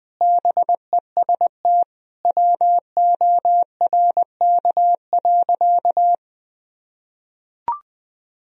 Morse code, heard as a keyed tone, about 20 wpm, 700 Hz. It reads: BEST WORK.
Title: Sentences from Top 1000 Words - Review Part 3 - 20wpm